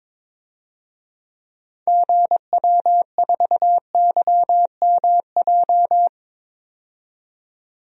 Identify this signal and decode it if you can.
Morse code, heard as a keyed tone, about 22 wpm, 700 Hz. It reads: GW4YMJ
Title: DXCC-100 Call Signs - 6 Characters - 22wpm